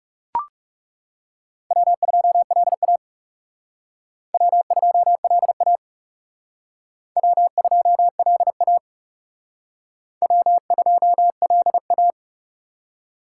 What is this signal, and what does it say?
Morse code, heard as a keyed tone, about 38 wpm, 700 Hz. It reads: W2LA W2LA W2LA W2LA
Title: Speed Racing - USA 1x2 Call Signs 30wpm